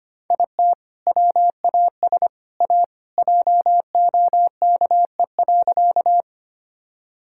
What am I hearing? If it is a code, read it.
Morse code, heard as a keyed tone, about 25 wpm, 700 Hz. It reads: IT WAS A JOKE.
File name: practice / 25wpm / Sentences with 3-Word Phrases - 25wpm